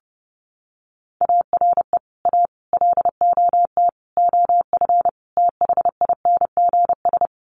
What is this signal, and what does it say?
Morse code, heard as a keyed tone, about 30 wpm, 700 Hz. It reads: ARE A LOT OF THINGS
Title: Sets of 5 Words Encore2 30wpm